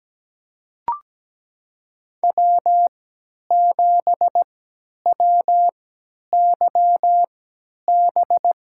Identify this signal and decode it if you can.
Morse code, heard as a keyed tone, about 17 wpm, spaced as for 10 wpm, 700 Hz. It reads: W7WYB
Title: USA Call Signs - 5 Characters - 3x Character Spacing - 17wpm